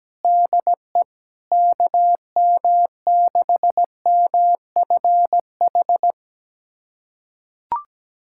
Morse code, heard as a keyed tone, about 17 wpm, 700 Hz. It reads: DE KM6MFH
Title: USA Callsign DE Callsign 17wpm